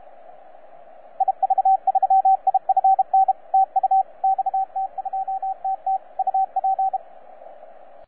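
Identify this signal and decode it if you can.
Morse code, heard as a keyed tone, about 32 wpm, 770 Hz, with radio noise. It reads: IV3IFN TU XT2TT UP
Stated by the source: XT2TT DX station on 12 meters CW, from DX World.net.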